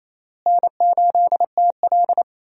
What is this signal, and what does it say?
Morse code, heard as a keyed tone, about 28 wpm, 700 Hz. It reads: N8TL
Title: USA Call Signs - 4 Characters - Repeated 3x - 28wpm